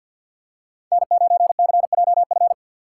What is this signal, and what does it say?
Morse code, heard as a keyed tone, about 49 wpm, 690 Hz. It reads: N9CJF
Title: USA Call Signs - 5 Characters - 3x Character Spacing - 50wpm